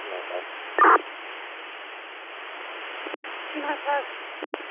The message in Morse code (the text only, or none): none